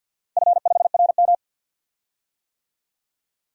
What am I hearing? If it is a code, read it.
Morse code, heard as a keyed tone, about 53 wpm, 700 Hz. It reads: F5RM
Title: DXCC-100 Call Signs - 4 Characters - 50wpm